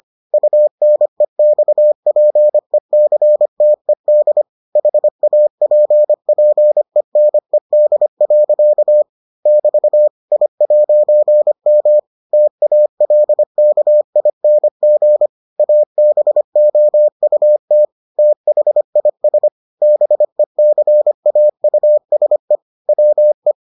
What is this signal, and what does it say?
Morse code, heard as a keyed tone, about 25 wpm, 600 Hz. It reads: UNEXPECTED HAPPENED. <BT> I'M TALKING ABOUT THIS BECAUSE WE